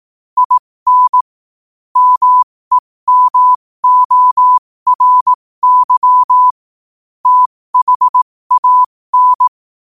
Morse code, heard as a keyed tone, about 18 wpm, spaced as for 15 wpm, 1000 Hz. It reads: IN MEMORY THAN
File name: qotdfiles / 15wpm